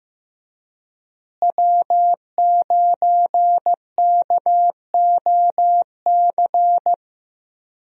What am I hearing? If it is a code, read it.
Morse code, heard as a keyed tone, about 15 wpm, 700 Hz. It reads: W9KOC
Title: USA Call Signs - 5 Characters - Repeated 3x - 15wpm